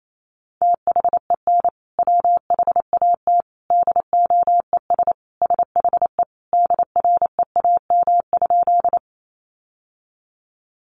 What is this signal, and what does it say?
Morse code, heard as a keyed tone, about 28 wpm, 700 Hz. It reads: THEN WHAT DOES SHE DREAM?
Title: Sentences from Top 1500 Words 28wpm